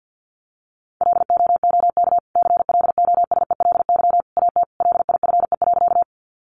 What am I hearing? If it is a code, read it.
Morse code, heard as a keyed tone, about 50 wpm, 700 Hz. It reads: LOOK CLOSELY AT LIFE.
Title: Sentences from Top 1400 Words 50wpm